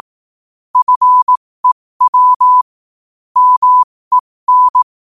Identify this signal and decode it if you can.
Morse code, heard as a keyed tone, about 18 wpm, spaced as for 15 wpm, 1000 Hz. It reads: FEW MEN